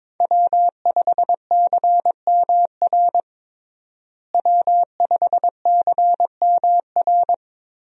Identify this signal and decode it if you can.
Morse code, heard as a keyed tone, about 22 wpm, 700 Hz. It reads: W5CMR W5CMR
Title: USA Call Signs - 5 Characters - Repeated 3x - 22wpm